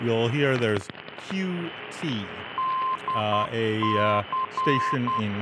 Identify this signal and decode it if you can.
Morse code, heard as a keyed tone, about 10 wpm, 1040 Hz, with speech and radio noise. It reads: DC